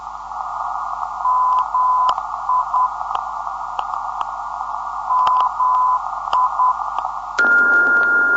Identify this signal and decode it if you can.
Morse code, heard as a keyed tone, about 10 wpm, 1010 Hz, with noise. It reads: MI MI
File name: ndb zvok / mi 355khz